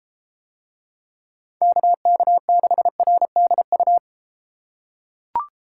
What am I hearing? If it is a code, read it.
Morse code, heard as a keyed tone, about 29 wpm, 700 Hz. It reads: KK6RDU